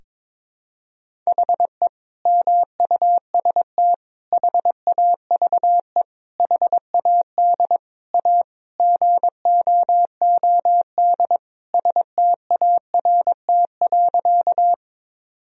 Morse code, heard as a keyed tone, about 21 wpm, 700 Hz. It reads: HE MUST HAVE HAD A GOOD START.
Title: Sentences from Top 1000 Words - Review Part 1 - 22wpm